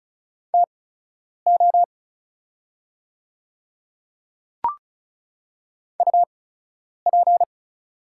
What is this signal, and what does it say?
Morse code, heard as a keyed tone, about 34 wpm, spaced as for 10 wpm, 690 Hz. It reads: TO UP
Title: 8x Character Spacing - Top 100 Words - 35wpm